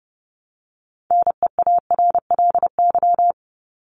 Morse code, heard as a keyed tone, about 30 wpm, 700 Hz. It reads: NEARLY
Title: LY Suffix - Top 100 Words with Base Words 30wpm